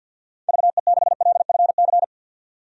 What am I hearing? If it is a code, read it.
Morse code, heard as a keyed tone, about 53 wpm, 700 Hz. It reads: VE6LFC
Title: DXCC-100 Call Signs - 6 Characters - 50wpm